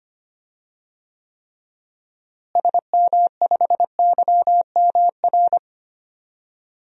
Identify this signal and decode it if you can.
Morse code, heard as a keyed tone, about 25 wpm, 700 Hz. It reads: SM5YMR
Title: DXCC-100 Call Signs - 6 Characters - Repeated 3x - 25wpm